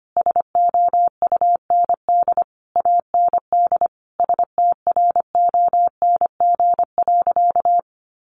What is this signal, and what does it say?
Morse code, heard as a keyed tone, about 25 wpm, 700 Hz. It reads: SOUND AND STRONG.
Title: Sentences from Top 1400 Words 25wpm